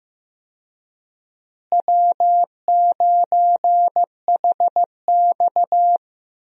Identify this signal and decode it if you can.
Morse code, heard as a keyed tone, about 16 wpm, 700 Hz. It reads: W9HX
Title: USA Call Signs - 4 Characters - Repeated 2x - 15wpm